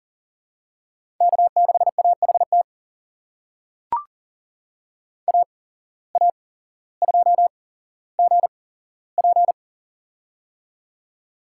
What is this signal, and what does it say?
Morse code, heard as a keyed tone, about 40 wpm, spaced as for 17 wpm, 700 Hz. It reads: K6AHT AA2GP
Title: USA Call Signs - 5 Characters - 8x Character Spacing - 40wpm